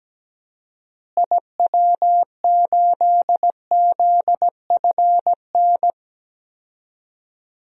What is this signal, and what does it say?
Morse code, heard as a keyed tone, about 17 wpm, 700 Hz. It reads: IW8ZFN
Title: DXCC-100 Call Signs - 6 Characters - Repeated 3x - 17wpm